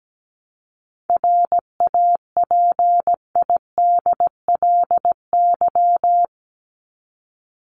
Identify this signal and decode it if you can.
Morse code, heard as a keyed tone, about 17 wpm, 700 Hz. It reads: RAPIDLY
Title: LY Suffix - Top 100 Words with Base Words 17wpm